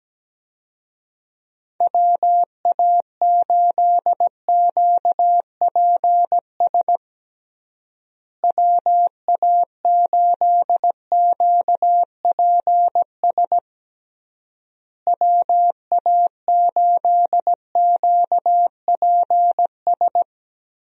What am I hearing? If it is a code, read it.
Morse code, heard as a keyed tone, about 17 wpm, 700 Hz. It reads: WA8QPS WA8QPS WA8QPS
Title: USA Call Signs - 6 Characters - Repeated 3x - 17wpm